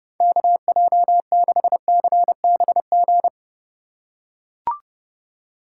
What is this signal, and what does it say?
Morse code, heard as a keyed tone, about 30 wpm, 700 Hz. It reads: KJ6CBG
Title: USA Call Signs - 6 Characters - 3x Character Spacing - 30wpm